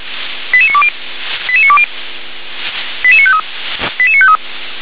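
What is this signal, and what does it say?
Five-tone selective call (ZVEI), heard as quick runs of five tones, tones of about 70 ms. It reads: 88920 88920 88953 88953
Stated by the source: ZVEI five-tone call received by SDR.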